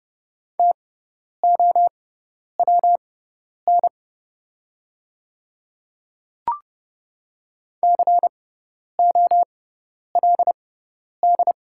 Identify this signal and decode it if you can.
Morse code, heard as a keyed tone, about 30 wpm, spaced as for 10 wpm, 700 Hz. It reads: TOWN COLD
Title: 6x Character Spacing - Top 500 Words - 30wpm